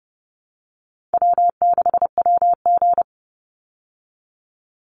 Morse code, heard as a keyed tone, about 30 wpm, 700 Hz. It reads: W6WG
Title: Easy Call Signs Repeated 3x - 30wpm